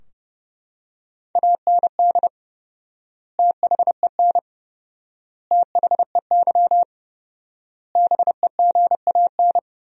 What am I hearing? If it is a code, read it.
Morse code, heard as a keyed tone, about 30 wpm, 700 Hz. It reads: AND THEN THEY BEGAN
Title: 4x Word-Spacing - Sentences from Top 300 Words - 30wpm